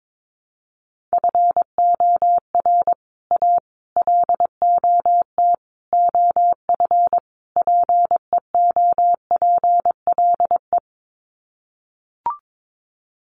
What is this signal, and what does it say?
Morse code, heard as a keyed tone, about 22 wpm, 700 Hz. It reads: FOR A LOT OF PEOPLE
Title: Sets of 5 Words Encore 22wpm